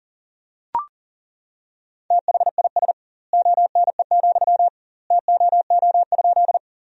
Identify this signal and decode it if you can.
Morse code, heard as a keyed tone, about 40 wpm, 700 Hz. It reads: THIS ONE, TOO?
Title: 2x Word-Spacing - Sentences from Top 200 Words - 40wpm